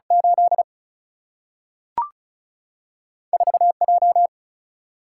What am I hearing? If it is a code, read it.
Morse code, heard as a keyed tone, about 36 wpm, 700 Hz. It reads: 8 4 J
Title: Single-Letter Single-Number 35wpm